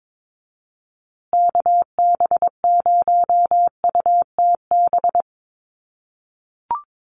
Morse code, heard as a keyed tone, about 22 wpm, 700 Hz. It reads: KB0UTB
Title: USA Call Signs Repeated 3x 22wpm